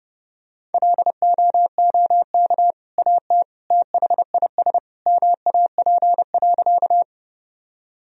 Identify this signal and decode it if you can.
Morse code, heard as a keyed tone, about 30 wpm, 700 Hz. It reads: LOOK AT THIS MAP.
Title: Sentences from Top 1000 Words - Review Part 2 - 30wpm